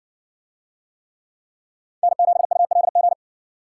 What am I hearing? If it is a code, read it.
Morse code, heard as a keyed tone, about 54 wpm, 700 Hz. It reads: N7ULG